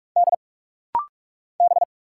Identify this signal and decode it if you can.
Morse code, heard as a keyed tone, about 46 wpm, 700 Hz. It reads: D B
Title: Single Letters - Rapid-Fire 45wpm